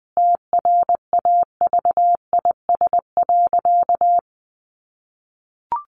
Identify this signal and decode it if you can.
Morse code, heard as a keyed tone, about 20 wpm, 700 Hz. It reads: TRAVIS.
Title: A Sound Of Thunder 20wpm